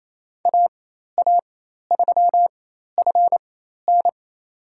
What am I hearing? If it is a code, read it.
Morse code, heard as a keyed tone, about 27 wpm, spaced as for 13 wpm, 700 Hz. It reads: AA3FN
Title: USA Call Signs - 4x Character Spacing - 28wpm